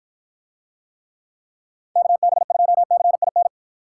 Morse code, heard as a keyed tone, about 49 wpm, 700 Hz. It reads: KB2CIN